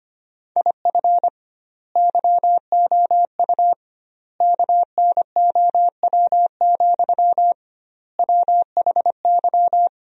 Morse code, heard as a keyed tone, about 25 wpm, 700 Hz. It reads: IF YOU KNOW, WHY
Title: 2x Word-Spacing - Sentences from Top 300 Words - 25wpm